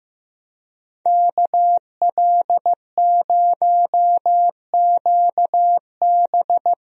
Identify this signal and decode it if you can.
Morse code, heard as a keyed tone, about 15 wpm, 700 Hz. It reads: KL0QB
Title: USA Call Signs - 5 Characters - 15wpm